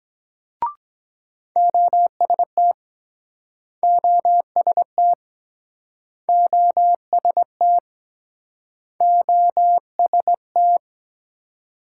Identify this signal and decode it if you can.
Morse code, heard as a keyed tone, about 23 wpm, 700 Hz. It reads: OST OST OST OST